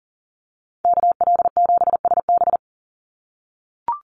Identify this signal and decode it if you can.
Morse code, heard as a keyed tone, about 40 wpm, 700 Hz. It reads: KL7SB
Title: Call Signs - 40wpm